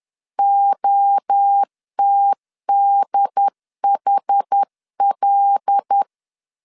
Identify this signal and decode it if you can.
Morse code, heard as a keyed tone, about 10 wpm, 800 Hz. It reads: OTDHL
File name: LEZIONI / 6 PKX0 / 6 PKX0 10wpm 2707